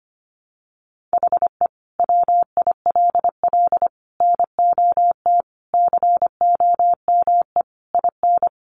Morse code, heard as a keyed tone, about 25 wpm, 700 Hz. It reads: HE WILL NOT COME IN
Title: Sentences from Top 600 Words 25wpm